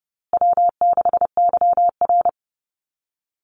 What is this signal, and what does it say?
Morse code, heard as a keyed tone, about 30 wpm, 700 Hz. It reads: W6YR
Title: Easy Call Signs Repeated 3x - 30wpm